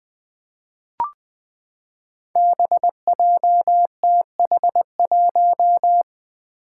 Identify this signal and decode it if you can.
Morse code, heard as a keyed tone, about 20 wpm, 700 Hz. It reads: BJTH1
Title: Five Challenging Letter-Number Combinations 20wpm